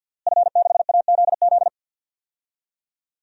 Morse code, heard as a keyed tone, about 52 wpm, 700 Hz. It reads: F6AZZ
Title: DXCC-100 Call Signs - 5 Characters - Repeated 3x - 50wpm